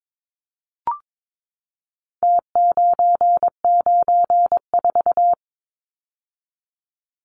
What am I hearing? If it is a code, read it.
Morse code, heard as a keyed tone, about 22 wpm, 700 Hz. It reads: T994
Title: Numbers - Contest Style - 4-Digit 22wpm